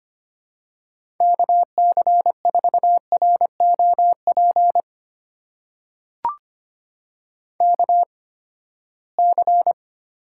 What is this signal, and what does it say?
Morse code, heard as a keyed tone, about 25 wpm, spaced as for 17 wpm, 700 Hz. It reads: KC4ROP KC
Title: USA Call Signs - 6 Characters - 8x Character Spacing - 25wpm